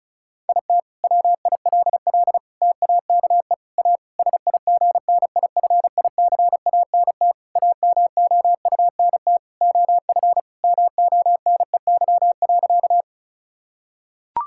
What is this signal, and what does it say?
Morse code, heard as a keyed tone, about 36 wpm, 700 Hz. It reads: IT WILL TAKE A SIGNIFICANT AMOUNT OF MONEY.
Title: Sentences from Top 1000 Words - Review Part 6 - 35wpm